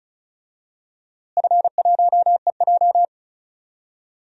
Morse code, heard as a keyed tone, about 36 wpm, 700 Hz. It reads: F1EJ
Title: DXCC-100 Call Signs - 4 Characters - Repeated 3x - 35wpm